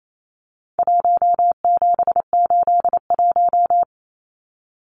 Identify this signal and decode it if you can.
Morse code, heard as a keyed tone, about 28 wpm, 700 Hz. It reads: 1781
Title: Numbers - Contest Style - 4-Digit 28wpm